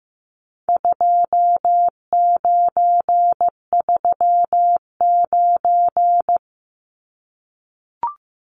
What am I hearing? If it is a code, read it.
Morse code, heard as a keyed tone, about 15 wpm, 700 Hz. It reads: 2939
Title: Numbers - 4-Digit 15wpm